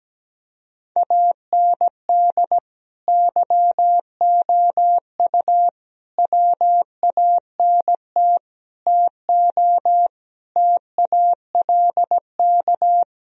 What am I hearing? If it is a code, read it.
Morse code, heard as a keyed tone, about 17 wpm, 700 Hz. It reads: AND YOU WANT TO TALK
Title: Sentences from Top 1000 Words - Review Part 4 - 17wpm